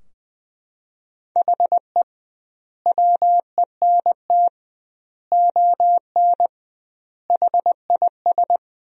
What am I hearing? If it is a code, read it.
Morse code, heard as a keyed tone, about 20 wpm, 700 Hz. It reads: HE WENT ON HIS
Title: 2x Word-Spacing - Sentences from Top 200 Words - 20wpm